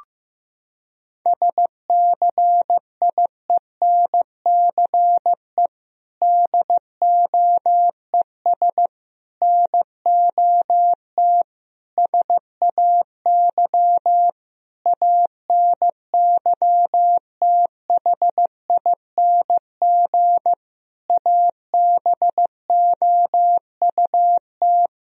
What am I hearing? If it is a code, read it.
Morse code, heard as a keyed tone, about 15 wpm, 700 Hz. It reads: SCIENCE DOES NOT SAY ANYTHING ABOUT
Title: Sentences from Top 1000 Words - Review Part 2 - 15wpm